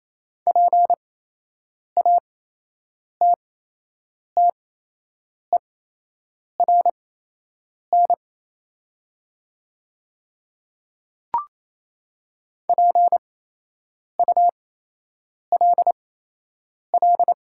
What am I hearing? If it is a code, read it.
Morse code, heard as a keyed tone, about 28 wpm, spaced as for 8 wpm, 700 Hz. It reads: PATTERN PULL
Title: 8x Character Spacing - Top 500 Words - 28wpm